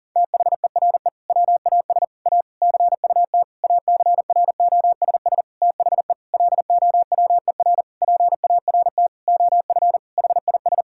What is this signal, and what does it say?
Morse code, heard as a keyed tone, about 40 wpm, 700 Hz. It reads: THERE WAS A CUT ACROSS THE LOWER PART OF HIS
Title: Sentences from Top 1000 Words - Review Part 5 - 40wpm